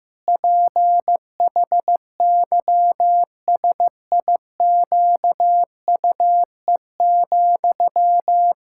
Morse code, heard as a keyed tone, about 15 wpm, 700 Hz. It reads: PHYSIQUE,
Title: Pangram Challenge v2 - 15wpm